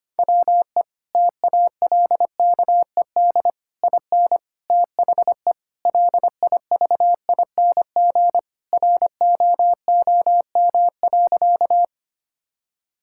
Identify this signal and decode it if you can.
Morse code, heard as a keyed tone, about 25 wpm, 700 Hz. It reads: WE TALKED IN THE LIVING ROOM.
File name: practice / 25wpm / Sentences with 4-Word Phrases - 25wpm